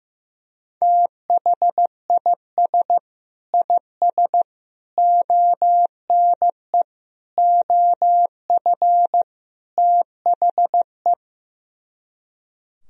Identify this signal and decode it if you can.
Morse code, heard as a keyed tone, about 15 wpm, 700 Hz. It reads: THIS IS ONE OF THE
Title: Speed-Racing - Sets of 5 Words 15wpm